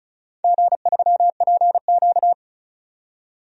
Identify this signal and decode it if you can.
Morse code, heard as a keyed tone, about 36 wpm, 690 Hz. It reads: G3PQ
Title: DXCC-100 Call Signs - 4 Characters - 35wpm